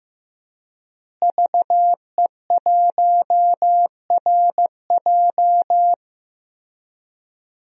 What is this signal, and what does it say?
Morse code, heard as a keyed tone, about 15 wpm, 700 Hz. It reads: VE1RJ
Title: DXCC-100 Call Signs - 5 Characters - Repeated 3x - 15wpm